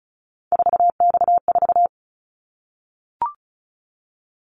Morse code, heard as a keyed tone, about 34 wpm, 700 Hz. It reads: 4X4
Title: Callsign Prefixes for Top 50 DXCC - 35wpm